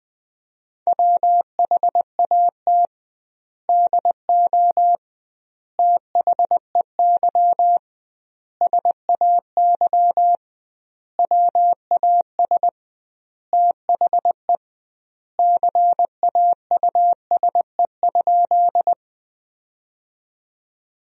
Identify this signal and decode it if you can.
Morse code, heard as a keyed tone, about 20 wpm, 700 Hz. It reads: WHAT DO THEY SAY WAS THE CAUSE?
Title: 2x Word-Spacing - Sentences from Top 200 Words - 20wpm